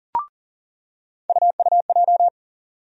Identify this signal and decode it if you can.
Morse code, heard as a keyed tone, about 40 wpm, 700 Hz. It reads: UUJ
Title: USA Call Sign Postfixes - 3 Letters 40wpm